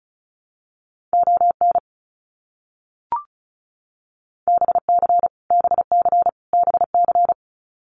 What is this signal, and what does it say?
Morse code, heard as a keyed tone, about 34 wpm, 700 Hz. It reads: ON BC BC BC
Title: Canadian Provinces and Territories - Repeated 3x - 35wpm